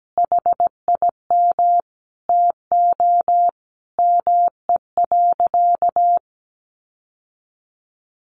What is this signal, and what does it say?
Morse code, heard as a keyed tone, about 17 wpm, 700 Hz. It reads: HIM TO ME.
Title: Sentences from Top 1500 Words 17wpm